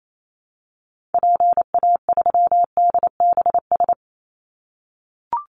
Morse code, heard as a keyed tone, about 28 wpm, 700 Hz. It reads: PA3DBS
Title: Call Signs - 28wpm